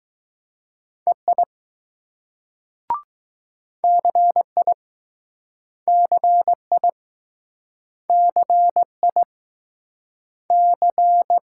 Morse code, heard as a keyed tone, about 20 wpm, 700 Hz. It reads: EI CI CI CI C